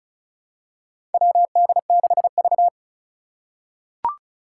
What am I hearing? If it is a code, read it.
Morse code, heard as a keyed tone, about 36 wpm, 700 Hz. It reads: WD6V
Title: Four Challenging Letter-Number Combinations 35wpm